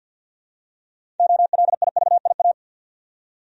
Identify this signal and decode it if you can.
Morse code, heard as a keyed tone, about 48 wpm, 700 Hz. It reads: OLIVIA
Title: USA Common Names - 50wpm